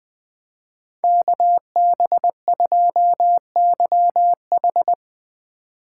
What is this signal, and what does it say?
Morse code, heard as a keyed tone, about 20 wpm, 700 Hz. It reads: KB2YH
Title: Speed Racing - USA 2x2 Call Signs 17wpm